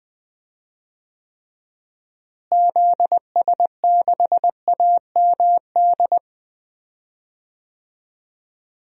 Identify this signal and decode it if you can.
Morse code, heard as a keyed tone, about 20 wpm, 700 Hz. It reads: ZS6AMD
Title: DXCC-100 Call Signs - 6 Characters - Repeated 2x - 20wpm